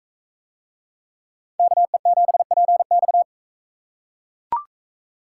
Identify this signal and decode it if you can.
Morse code, heard as a keyed tone, about 42 wpm, 700 Hz. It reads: KE7PX